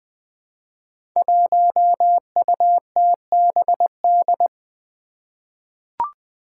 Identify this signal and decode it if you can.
Morse code, heard as a keyed tone, about 20 wpm, 700 Hz. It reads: 1UTBD
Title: Five Challenging Letter-Number Combinations 20wpm